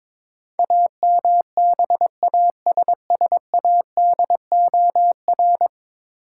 Morse code, heard as a keyed tone, about 21 wpm, 700 Hz. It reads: AMBASSADOR
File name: practice / 22wpm / 10 Letter Words - First 3 Letters Given - 22wpm